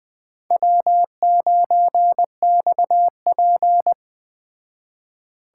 Morse code, heard as a keyed tone, about 20 wpm, 700 Hz. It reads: W9XP
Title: Speed Racing - USA Call Signs 17wpm